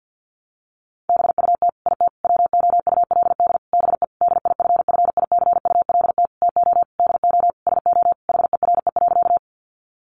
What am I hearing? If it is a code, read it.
Morse code, heard as a keyed tone, about 50 wpm, 700 Hz. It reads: BUT IT WOULD BE DIFFICULT TO DO SO HERE.
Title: Sentences from Top 600 Words 50wpm